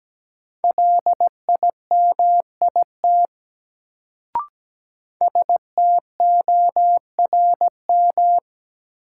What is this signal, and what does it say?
Morse code, heard as a keyed tone, about 17 wpm, 700 Hz. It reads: LIMIT STORM
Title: Sentences from Top 1500 Words - New Word Drill - 17wpm